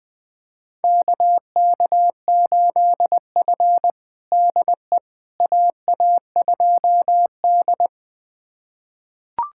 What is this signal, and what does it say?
Morse code, heard as a keyed tone, about 20 wpm, 700 Hz. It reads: KK8F DE AA2D
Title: USA Callsign DE Callsign - 4 Characters - 20wpm